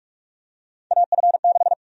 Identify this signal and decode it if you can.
Morse code, heard as a keyed tone, about 46 wpm, 700 Hz. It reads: AF6
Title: USA Call Sign Prefixes - Repeated 2x - 45wpm